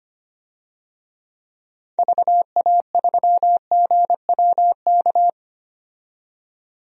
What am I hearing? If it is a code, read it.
Morse code, heard as a keyed tone, about 25 wpm, 700 Hz. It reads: VA3GWK
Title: DXCC-100 Call Signs - 6 Characters - 25wpm